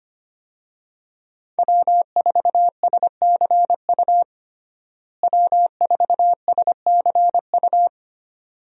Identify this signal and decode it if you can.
Morse code, heard as a keyed tone, about 25 wpm, 700 Hz. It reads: W4SCU W4SCU
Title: USA Call Signs - 5 Characters - Repeated 2x - 25wpm